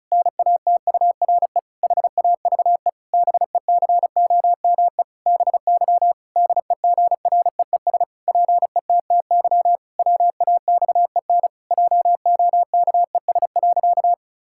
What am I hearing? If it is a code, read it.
Morse code, heard as a keyed tone, about 36 wpm, 700 Hz. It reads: NATURE HAVE BECOME BY DEGREES PETTY WAXEN JOKES.
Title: Pangram Challenge v2 - 35wpm